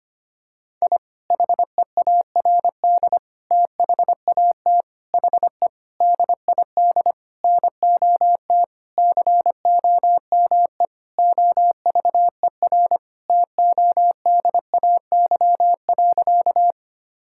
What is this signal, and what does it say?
Morse code, heard as a keyed tone, about 25 wpm, 700 Hz. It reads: I HEARD THAT HE DID NOT COME OVER TODAY.
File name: practice / 25wpm / Sentences with 4-Word Phrases - 25wpm